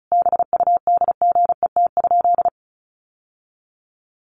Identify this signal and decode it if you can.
Morse code, heard as a keyed tone, about 34 wpm, 700 Hz. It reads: BUDGET?
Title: Sentences from Top 1200 Words 35wpm